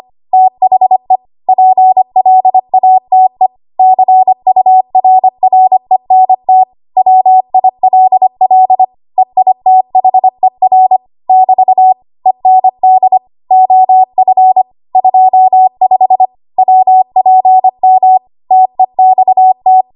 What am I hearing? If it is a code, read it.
Morse code, heard as a keyed tone, about 25 wpm, 750 Hz. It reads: THE PLATE CURRENT WILL EITHER <BT> END OF 25 WPM TEXT
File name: Archive / 25 WPM / 25WPM